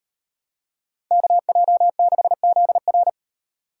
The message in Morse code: KJ6ZR

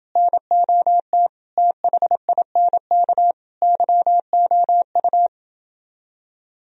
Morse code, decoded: NOT THINK YOU

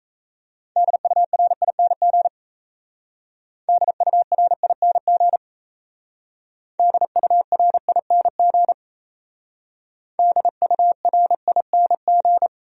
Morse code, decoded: DURING DURING DURING DURING

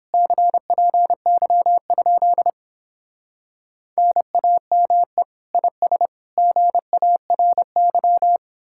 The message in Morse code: CPY? NAME IS GARY

